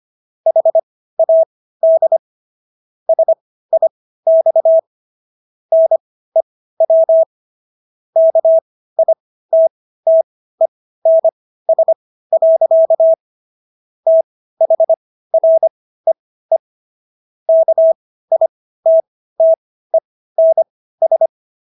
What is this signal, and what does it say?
Morse code, heard as a keyed tone, about 25 wpm, spaced as for 15 wpm, 650 Hz. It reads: HAD SIX NEW KITTENS. THREE KITTENS